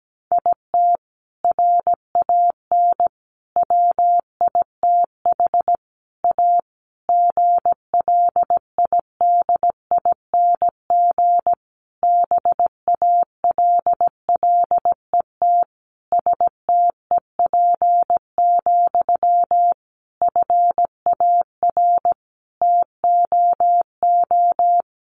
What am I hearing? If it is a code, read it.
Morse code, heard as a keyed tone, about 17 wpm, 700 Hz. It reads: IT RAN WITH A GLIDING BALLET STEP, FAR TOO
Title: A Sound Of Thunder 17wpm